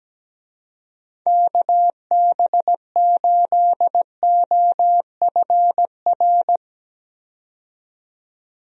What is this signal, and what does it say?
Morse code, heard as a keyed tone, about 17 wpm, 700 Hz. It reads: KB8OFR
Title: USA Call Signs - 6 Characters - 17wpm